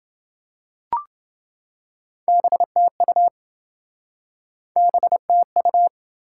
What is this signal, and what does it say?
Morse code, heard as a keyed tone, about 29 wpm, 700 Hz. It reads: BTU BTU